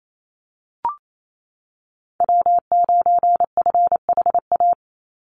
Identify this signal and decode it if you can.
Morse code, heard as a keyed tone, about 28 wpm, 700 Hz. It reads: W9FHA
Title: NA SOTA Chaser Callsigns 28wpm